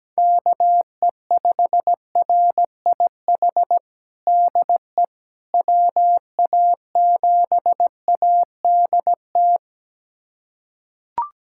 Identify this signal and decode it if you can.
Morse code, heard as a keyed tone, about 17 wpm, 700 Hz. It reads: KE5RIH DE WA7ADT
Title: USA Callsign DE Callsign - 6 Characters - 17wpm